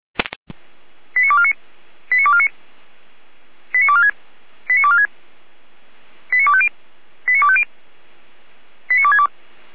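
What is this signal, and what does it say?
Five-tone selective call (ZVEI), heard as quick runs of five tones, tones of about 70 ms. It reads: 89249 89249 89246 89246 89250 89250 89262
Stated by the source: ZVEI five-tone call received by SDR, from feuerwehr-orlamuende.org (fire brigade Orlamünde).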